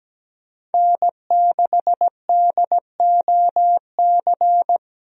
Morse code, heard as a keyed tone, about 17 wpm, 700 Hz. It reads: N6DOC
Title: USA Call Signs - 5 Characters - Repeated 2x - 17wpm